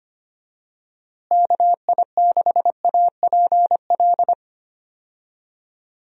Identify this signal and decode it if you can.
Morse code, heard as a keyed tone, about 25 wpm, 700 Hz. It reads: KI6APL
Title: USA Call Signs - 6 Characters - Repeated 3x - 25wpm